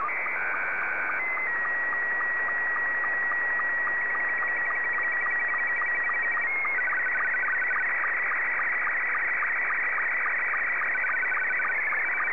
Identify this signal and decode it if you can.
SSTV (slow-scan television), Scottie 2.